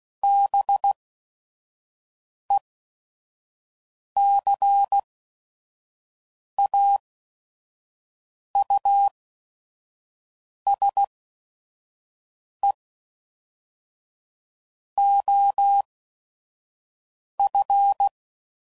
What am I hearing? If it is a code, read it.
Morse code, heard as a keyed tone, about 16 wpm, spaced as for 5 wpm, 800 Hz. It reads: BECAUSE OF
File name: CW Code Groups @5WPM